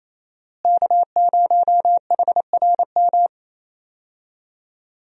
Morse code, heard as a keyed tone, about 28 wpm, 700 Hz. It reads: K0HRM
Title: USA Call Signs - 5 Characters - Repeated 2x - 28wpm